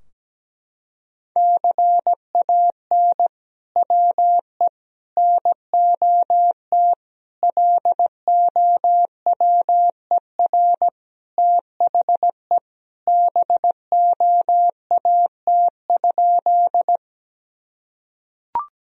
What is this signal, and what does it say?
Morse code, heard as a keyed tone, about 17 wpm, 700 Hz. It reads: CAN WE NOT LOWER THE BOAT?
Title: Sentences from Top 1000 Words - Review Part 4 - 17wpm